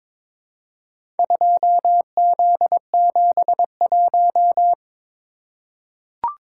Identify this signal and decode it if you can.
Morse code, heard as a keyed tone, about 22 wpm, 700 Hz. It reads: 2Z71
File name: practice / 22wpm / Four Letter-Number Combinations - 22wpm